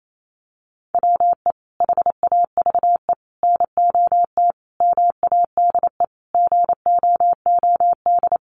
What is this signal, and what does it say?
Morse code, heard as a keyed tone, about 28 wpm, 700 Hz. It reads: WE HAVE NOT MADE GOOD
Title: Sentences from Top 1400 Words 28wpm